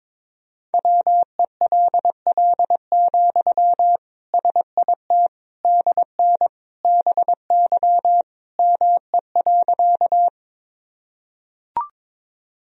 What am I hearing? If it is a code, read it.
Morse code, heard as a keyed tone, about 21 wpm, 700 Hz. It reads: WELL, SIT DN BY ME.
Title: Sentences from Top 500 Words and Abbreviations 22wpm